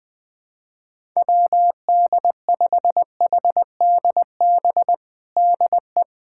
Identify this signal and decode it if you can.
Morse code, heard as a keyed tone, about 20 wpm, 700 Hz. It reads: WD5HDB DE
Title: USA Callsign DE Callsign - 6 Characters - 20wpm